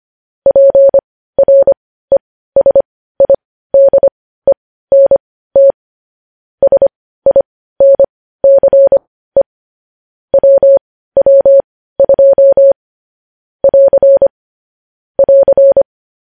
Morse code, heard as a keyed tone, about 25 wpm, spaced as for 15 wpm, 550 Hz. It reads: PRESIDENT SINCE WW2 <AR> <AR>
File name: podcast / News headlines-1x-25x15